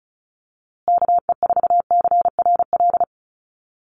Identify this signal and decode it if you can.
Morse code, heard as a keyed tone, about 34 wpm, 700 Hz. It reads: KE4CRL
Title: USA Call Signs Repeated 3x 35wpm